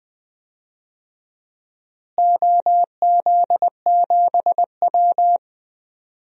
Morse code, heard as a keyed tone, about 21 wpm, 700 Hz. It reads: OZ7W
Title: DXCC-100 Call Signs - 4 Characters - Repeated 2x - 20wpm